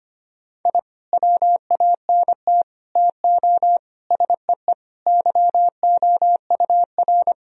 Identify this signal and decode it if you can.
Morse code, heard as a keyed tone, about 25 wpm, 700 Hz. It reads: I WANT TO SEE YOUR
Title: Sentences from Top 1000 Words - Review Part 1 - 25wpm